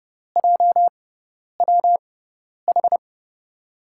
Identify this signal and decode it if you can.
Morse code, heard as a keyed tone, about 30 wpm, spaced as for 10 wpm, 700 Hz. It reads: JWH
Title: USA Call Signs - 5 Characters - 6x Character Spacing - 30wpm